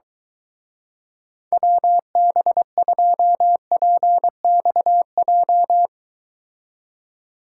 Morse code, heard as a keyed tone, about 23 wpm, 700 Hz. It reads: WB2PXJ